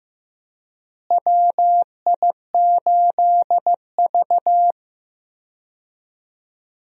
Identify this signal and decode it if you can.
Morse code, heard as a keyed tone, about 15 wpm, 700 Hz. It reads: WI8V
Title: USA Call Signs - 4 Characters - Repeated 3x - 15wpm